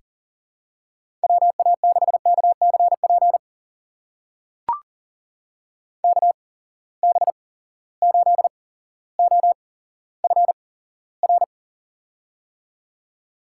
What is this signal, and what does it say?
Morse code, heard as a keyed tone, about 37 wpm, spaced as for 17 wpm, 700 Hz. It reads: WA6KCP KB8OFR
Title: USA Call Signs - 6 Characters - 8x Character Spacing - 40wpm